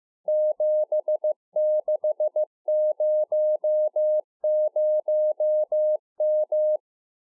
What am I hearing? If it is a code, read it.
Morse code, heard as a keyed tone, about 15 wpm, 610 Hz. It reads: 7600M
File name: 15WPM